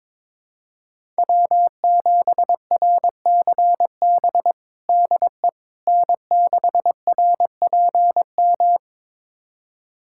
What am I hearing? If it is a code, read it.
Morse code, heard as a keyed tone, about 22 wpm, 700 Hz. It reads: W7RCB DE N6RPM
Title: USA Callsign DE Callsign - 5 Characters - 22wpm